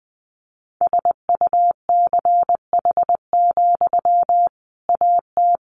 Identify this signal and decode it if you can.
Morse code, heard as a keyed tone, about 20 wpm, 700 Hz. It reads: SUCH, AT
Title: Sentences from Top 1500 Words 20wpm